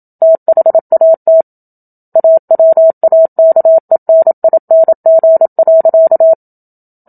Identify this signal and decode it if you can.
Morse code, heard as a keyed tone, about 27 wpm, 650 Hz. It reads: THAT AWAKENING.
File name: Book Two / 01-Chapter 1 - 27 WPM